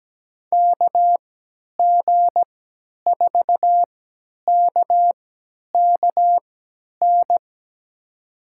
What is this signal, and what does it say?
Morse code, heard as a keyed tone, about 17 wpm, spaced as for 10 wpm, 700 Hz. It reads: KG4KKN
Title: USA Call Signs - 6 Characters - 3x Character Spacing - 17wpm